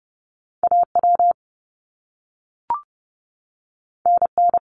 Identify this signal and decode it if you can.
Morse code, heard as a keyed tone, about 30 wpm, 700 Hz. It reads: AW NN
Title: Top Two Letter Combinations in Words 30wpm